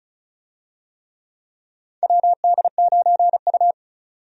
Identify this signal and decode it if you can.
Morse code, heard as a keyed tone, about 34 wpm, 700 Hz. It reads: WD9U
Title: USA Call Signs - 4 Characters - 35wpm